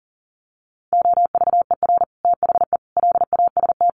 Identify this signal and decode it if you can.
Morse code, heard as a keyed tone, about 40 wpm, 700 Hz. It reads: OVER THE LAST
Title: Sets of 5 Words Encore 40wpm